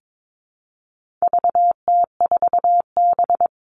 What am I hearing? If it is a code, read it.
Morse code, heard as a keyed tone, about 22 wpm, 700 Hz. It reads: VT4B
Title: All Cut Numbers - 4-Digit 22wpm